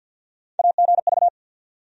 Morse code, heard as a keyed tone, about 53 wpm, 700 Hz. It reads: AGV
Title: USA Call Sign Postfixes - 3 Letters 50wpm